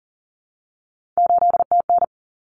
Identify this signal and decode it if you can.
Morse code, heard as a keyed tone, about 40 wpm, 700 Hz. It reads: 8TN
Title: Common Cut Numbers - 3-Digit 40wpm